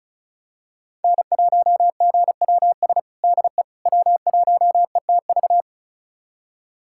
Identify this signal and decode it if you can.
Morse code, heard as a keyed tone, about 36 wpm, 700 Hz. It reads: N1GWS DE W1ETV
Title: USA Callsign DE Callsign - 5 Characters - 35wpm